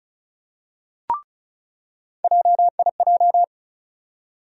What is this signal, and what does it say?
Morse code, heard as a keyed tone, about 36 wpm, 700 Hz. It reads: JIJ
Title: USA Call Sign Postfixes - 3 Letters 35wpm